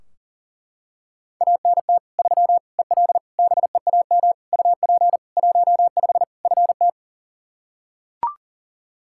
Morse code, heard as a keyed tone, about 37 wpm, 700 Hz. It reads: ANT 3 EL BEAM UP 15 FT
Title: K1USN SST QSO Practice 40wpm